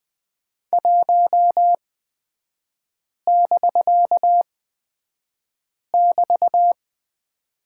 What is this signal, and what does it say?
Morse code, heard as a keyed tone, about 20 wpm, spaced as for 5 wpm, 700 Hz. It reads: 1<BK><BT>
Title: 250 Lesson 40e - BK - Three Characters 20wpm